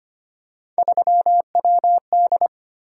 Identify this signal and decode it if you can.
Morse code, heard as a keyed tone, about 25 wpm, 700 Hz. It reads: 3WD
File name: Three Letter-Number Combinations - 25wpm